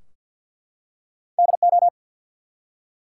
Morse code, heard as a keyed tone, about 48 wpm, 700 Hz. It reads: DO
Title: Two Letter Words - 50wpm